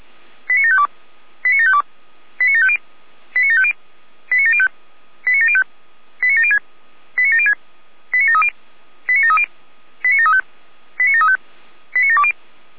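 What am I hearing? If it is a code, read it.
Five-tone selective call (ZVEI), heard as quick runs of five tones, tones of about 70 ms. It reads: 89752 89752 89750 89750 89705 89705 89706 89706 89730 89730 89735 89735 89720
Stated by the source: ZVEI five-tone selective call received on SDR.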